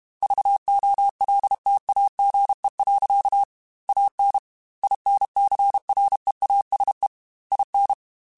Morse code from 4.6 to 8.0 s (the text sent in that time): INCREASE IN